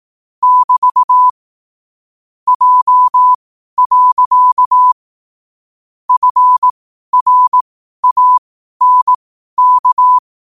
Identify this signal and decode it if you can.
Morse code, heard as a keyed tone, about 18 wpm, spaced as for 12 wpm, 1000 Hz. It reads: <BT> J. FRANK
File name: qotdfiles / 12wpm